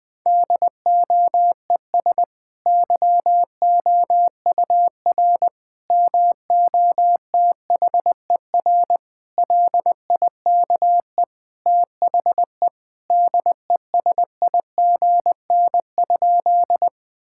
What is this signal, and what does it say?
Morse code, heard as a keyed tone, about 20 wpm, 700 Hz. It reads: DOES YOUR MOTHER LIKE THE DESIGN?
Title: Sentences from Top 1000 Words - Review Part 5 - 20wpm